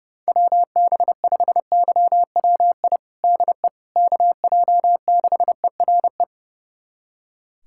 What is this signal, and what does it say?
Morse code, heard as a keyed tone, about 30 wpm, 700 Hz. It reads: WB5YWI DE KJ6ERE
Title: USA Callsign DE Callsign - 6 Characters - 30wpm